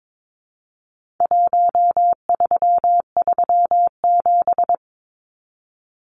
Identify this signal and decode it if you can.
Morse code, heard as a keyed tone, about 22 wpm, 700 Hz. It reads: 1337
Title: Numbers - 4-Digit 22wpm